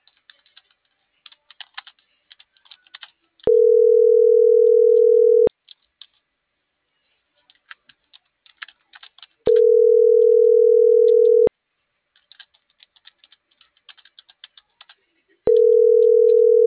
Ringback tone (North America), 440 + 480 Hz.